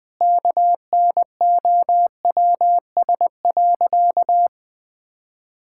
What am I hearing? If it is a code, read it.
Morse code, heard as a keyed tone, about 20 wpm, 700 Hz. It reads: KNOWS.
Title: Sentences with 5-Word Phrases - 20wpm